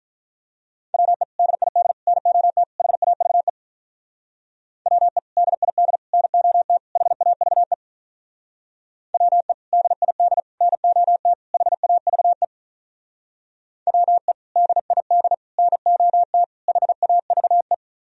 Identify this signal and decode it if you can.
Morse code, heard as a keyed tone, about 46 wpm, 700 Hz. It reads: WE DID NOT HAVE WE DID NOT HAVE WE DID NOT HAVE WE DID NOT HAVE